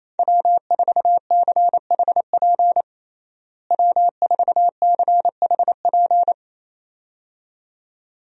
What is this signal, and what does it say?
Morse code, heard as a keyed tone, about 28 wpm, 700 Hz. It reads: W4CHP W4CHP
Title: USA Call Signs - 5 Characters - Repeated 2x - 28wpm